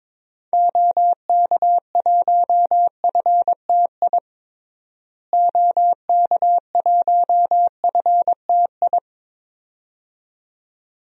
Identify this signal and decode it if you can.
Morse code, heard as a keyed tone, about 22 wpm, 700 Hz. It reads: OK1FTI OK1FTI
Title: DXCC-100 Call Signs - 6 Characters - Repeated 2x - 22wpm